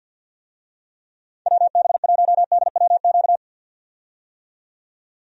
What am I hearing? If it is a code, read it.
Morse code, heard as a keyed tone, about 46 wpm, 700 Hz. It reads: WB1DWQ
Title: USA Call Signs - 6 Characters - 50wpm